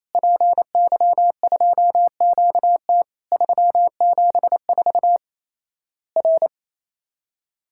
Morse code, heard as a keyed tone, about 28 wpm, 700 Hz. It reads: PY2QT 374 R
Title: Worked All Europe DX Contest - QTC Exchange - 28wpm